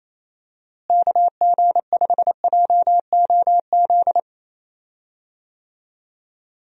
Morse code, heard as a keyed tone, about 28 wpm, 700 Hz. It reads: KG5JOZ
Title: USA Call Signs - 6 Characters - Repeated 2x - 28wpm